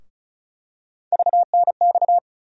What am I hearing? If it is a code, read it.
Morse code, heard as a keyed tone, about 36 wpm, 700 Hz. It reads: VNX
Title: USA Call Sign Postfixes - 3 Letters 35wpm